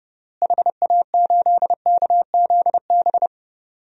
Morse code, heard as a keyed tone, about 30 wpm, 700 Hz. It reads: HA8KZB
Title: DXCC-100 Call Signs - 6 Characters - Repeated 3x - 30wpm